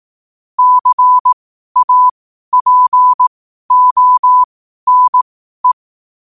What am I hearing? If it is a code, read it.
Morse code, heard as a keyed tone, about 18 wpm, spaced as for 12 wpm, 1000 Hz. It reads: CAPONE
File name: qotdfiles / 12wpm